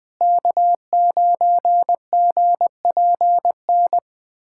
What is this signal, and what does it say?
Morse code, heard as a keyed tone, about 20 wpm, 700 Hz. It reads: K9GPN
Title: USA Callsign DE Callsign - 5 Characters - 20wpm